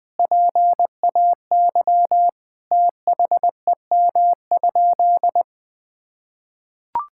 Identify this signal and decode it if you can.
Morse code, heard as a keyed tone, about 21 wpm, 700 Hz. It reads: PAY THEM?
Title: Sentences from Top 1000 Words - Review Part 5 - 20wpm